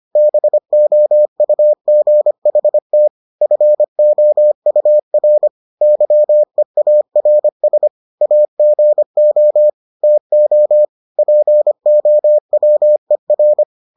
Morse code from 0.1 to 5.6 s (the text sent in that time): BOUGHT FOUR